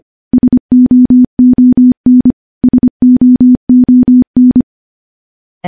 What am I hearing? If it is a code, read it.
Morse code, heard as a keyed tone, about 25 wpm, 250 Hz. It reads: SOON SOON
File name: morse-training-25WPM